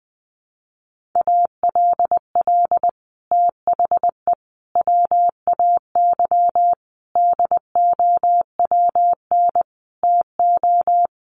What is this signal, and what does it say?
Morse code, heard as a keyed tone, about 20 wpm, 700 Hz. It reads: ALL THE WAY DOWN TO
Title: Sets of 5 Words Encore2 20wpm